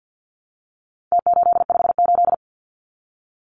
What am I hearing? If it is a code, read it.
Morse code, heard as a keyed tone, about 48 wpm, 700 Hz. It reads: T858
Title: Numbers - Contest Style - 4-Digit 50wpm